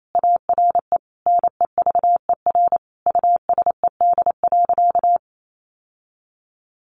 Morse code, heard as a keyed tone, about 28 wpm, 700 Hz. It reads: ARE NEVER USED.
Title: Sentences from Top 600 Words 28wpm